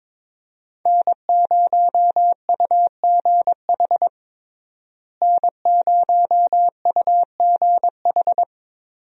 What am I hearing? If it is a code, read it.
Morse code, heard as a keyed tone, about 21 wpm, 700 Hz. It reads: N0UGH N0UGH
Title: USA Call Signs - 5 Characters - Repeated 3x - 22wpm